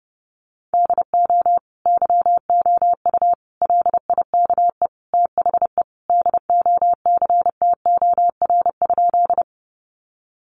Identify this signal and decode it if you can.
Morse code, heard as a keyed tone, about 28 wpm, 700 Hz. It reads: DO YOU LIKE THE DOCTOR?
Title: Sentences from Top 1500 Words 30wpm